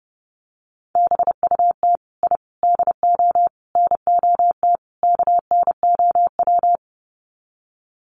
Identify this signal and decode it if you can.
Morse code, heard as a keyed tone, about 30 wpm, 700 Hz. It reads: BUT I DO NOT KNOW
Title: Sets of 5 Words 30wpm